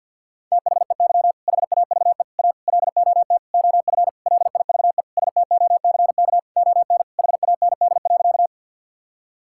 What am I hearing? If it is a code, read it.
Morse code, heard as a keyed tone, about 50 wpm, 700 Hz. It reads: THEY HAVE A LOT OF LIVE STOCK ON HAND.
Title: Sentences from Top 1000 Words - Review Part 5 - 50wpm